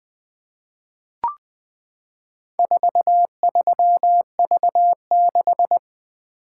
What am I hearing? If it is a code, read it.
Morse code, heard as a keyed tone, about 20 wpm, 700 Hz. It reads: 43V6
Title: Four Letter-Number Combinations - 20wpm